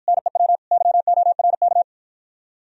Morse code, heard as a keyed tone, about 52 wpm, 700 Hz. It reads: NEW YORK